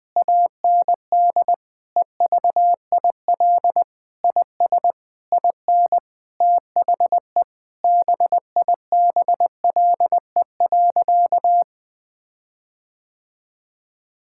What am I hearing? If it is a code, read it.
Morse code, heard as a keyed tone, about 20 wpm, 700 Hz. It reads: AND EVIL IS IN THE BIBLE.
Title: Sentences with 9-Word Phrases - Spoken Beginning - 20wpm